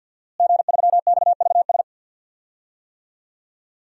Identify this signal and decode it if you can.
Morse code, heard as a keyed tone, about 48 wpm, 700 Hz. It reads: G3XVS
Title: DXCC-100 Call Signs - 5 Characters - Repeated 2x - 50wpm